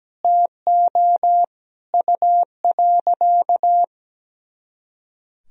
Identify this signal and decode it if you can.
Morse code, heard as a keyed tone, about 17 wpm, 700 Hz. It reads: TO U.
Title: Sentences from Top 500 Words and Abbreviations 17wpm